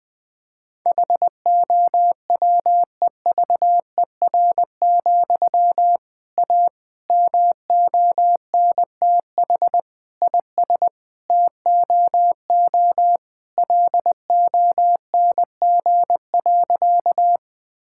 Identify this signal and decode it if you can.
Morse code, heard as a keyed tone, about 20 wpm, 700 Hz. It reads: HOWEVER, A MONTH IS TOO LONG.
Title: Sentences from Top 1000 Words - Review Part 5 - 20wpm